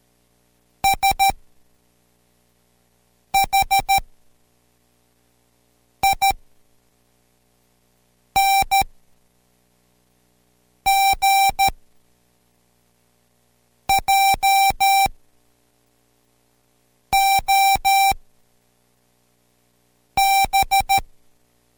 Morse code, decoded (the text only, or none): SHING JOB